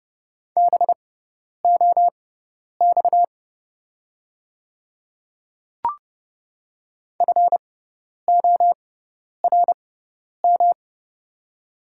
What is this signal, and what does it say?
Morse code, heard as a keyed tone, about 30 wpm, spaced as for 10 wpm, 690 Hz. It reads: BOX FORM
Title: 6x Character Spacing - Top 500 Words - 30wpm